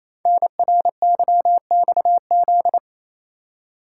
Morse code, heard as a keyed tone, about 28 wpm, 700 Hz. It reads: NRYXZ
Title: Code Groups - 5 Letters - 28wpm